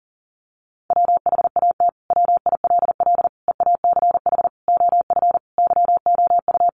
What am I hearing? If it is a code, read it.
Morse code, heard as a keyed tone, about 37 wpm, 700 Hz. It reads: WHAT WILL EACH OF YOU
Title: Sentences from Top 200 Words 40wpm